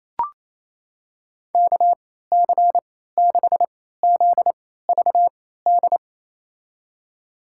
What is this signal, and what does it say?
Morse code, heard as a keyed tone, about 28 wpm, spaced as for 16 wpm, 700 Hz. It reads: KC6ZVD
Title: USA Call Signs - 6 Characters - 3x Character Spacing - 28wpm